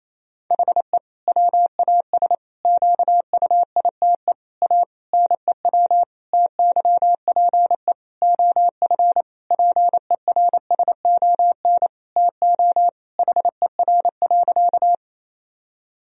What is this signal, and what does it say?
Morse code, heard as a keyed tone, about 28 wpm, 700 Hz. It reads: HE WAS QUITE A NEW TYPE OF PERSON TO HER.
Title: Sentences from Top 1000 Words - Review Part 3 - 28wpm